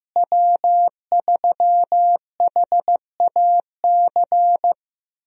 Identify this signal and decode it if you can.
Morse code, heard as a keyed tone, about 15 wpm, 700 Hz. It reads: W3HAC
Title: USA Call Signs - 5 Characters - Repeated 3x - 15wpm